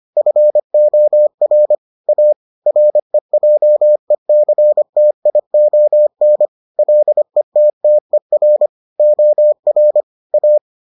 Morse code, heard as a keyed tone, about 25 wpm, 600 Hz. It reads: FOR A REJECTION LETTER OR A